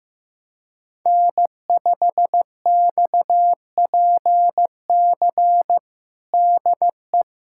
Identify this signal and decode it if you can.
Morse code, heard as a keyed tone, about 16 wpm, 700 Hz. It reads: N5XPC DE
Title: USA Callsign DE Callsign - 5 Characters - 15wpm